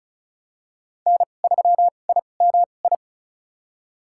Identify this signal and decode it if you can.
Morse code, heard as a keyed tone, about 36 wpm, spaced as for 25 wpm, 700 Hz. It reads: N3IMI
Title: USA Call Signs - 5 Characters - 2x Character Spacing - 35wpm